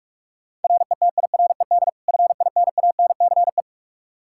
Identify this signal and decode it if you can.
Morse code, heard as a keyed tone, about 48 wpm, 700 Hz. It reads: RETIRED FINANCE
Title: K1USN SST QSO Practice 45wpm